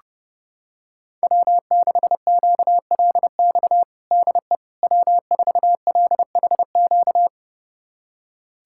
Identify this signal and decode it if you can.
Morse code, heard as a keyed tone, about 30 wpm, 700 Hz. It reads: W6QLX DE W4LHQ
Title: USA Callsign DE Callsign - 5 Characters - 30wpm